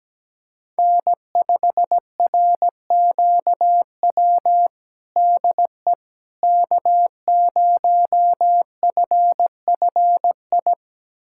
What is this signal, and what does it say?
Morse code, heard as a keyed tone, about 17 wpm, 700 Hz. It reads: N5RQW DE K0FFI
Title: USA Callsign DE Callsign 17wpm